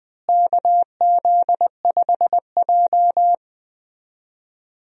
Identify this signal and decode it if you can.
Morse code, heard as a keyed tone, about 20 wpm, 700 Hz. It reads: KZ5J
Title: USA Call Signs - 4 Characters - Repeated 2x - 20wpm